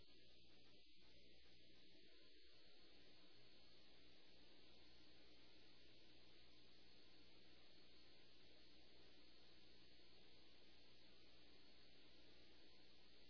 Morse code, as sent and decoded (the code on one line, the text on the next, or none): none